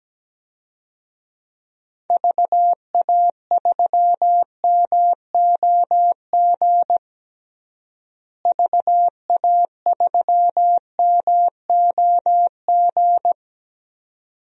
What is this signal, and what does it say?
Morse code, heard as a keyed tone, about 17 wpm, 700 Hz. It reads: VA3MOG VA3MOG
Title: DXCC-100 Call Signs - 6 Characters - Repeated 2x - 17wpm